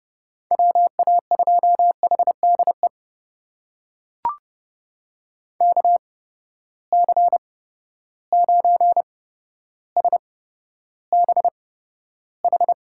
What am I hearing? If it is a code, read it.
Morse code, heard as a keyed tone, about 30 wpm, spaced as for 13 wpm, 700 Hz. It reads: WA2HDE KC9SBH
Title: USA Call Signs - 6 Characters - 8x Character Spacing - 30wpm